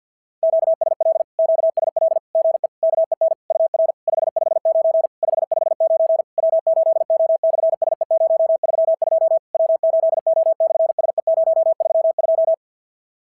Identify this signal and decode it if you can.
Morse code, heard as a keyed tone, about 46 wpm, 660 Hz. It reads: QSL QSL GE KEN UR 559 559 W8O/SE032 W8O/SE032
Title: SOTA QSO Practice 50wpm